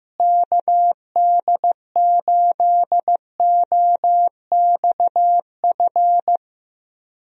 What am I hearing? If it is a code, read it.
Morse code, heard as a keyed tone, about 16 wpm, 700 Hz. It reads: KD8OXF
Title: USA Call Signs - 6 Characters - Repeated 3x - 15wpm